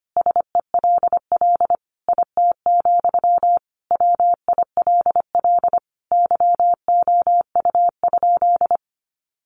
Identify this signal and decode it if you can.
Morse code, heard as a keyed tone, about 25 wpm, 700 Hz. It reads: SELL IT, WILL YOU?
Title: Sentences from Top 1100 Words 25wpm